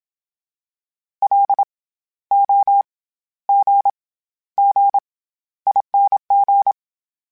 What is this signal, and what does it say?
Morse code, heard as a keyed tone, about 27 wpm, 800 Hz. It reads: L O G G ING
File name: Special INGs-mp3s / Special INGs-mp3s / SlowToFaster ing @25